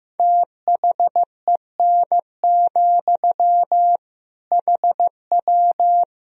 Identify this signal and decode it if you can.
Morse code, heard as a keyed tone, about 15 wpm, 700 Hz. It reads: THEN, HW
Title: Sentences from Top 100 Words and Abbreviations 15wpm